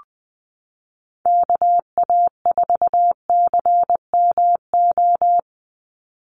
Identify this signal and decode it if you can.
Morse code, heard as a keyed tone, about 20 wpm, 700 Hz. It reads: KA4CMO
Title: USA Call Signs 20wpm